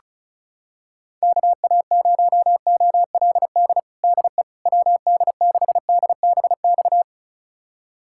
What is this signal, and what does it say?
Morse code, heard as a keyed tone, about 34 wpm, 700 Hz. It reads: KA0OLD DE WD6DBX
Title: USA Callsign DE Callsign - 6 Characters - 35wpm